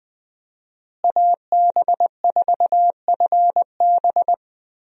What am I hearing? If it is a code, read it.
Morse code, heard as a keyed tone, about 20 wpm, 700 Hz. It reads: AB4FB
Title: USA Call Signs - 5 Characters - Repeated 3x - 20wpm